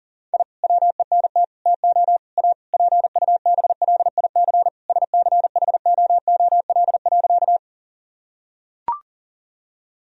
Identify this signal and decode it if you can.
Morse code, heard as a keyed tone, about 37 wpm, 700 Hz. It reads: I WENT TO A PUBLIC SCHOOL.
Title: Sentences from Top 1000 Words - Review Part 3 - 40wpm